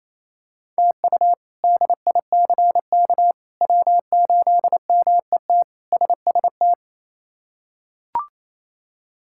Morse code, heard as a keyed tone, about 28 wpm, 700 Hz. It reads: TU DICK W8MET SST
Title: K1USN SST QSO Practice - 28wpm